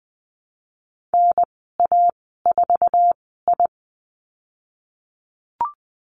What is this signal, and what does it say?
Morse code, heard as a keyed tone, about 20 wpm, spaced as for 15 wpm, 700 Hz. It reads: NA4I
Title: 055 Lesson 10g - Letter r - Call Signs 20wpm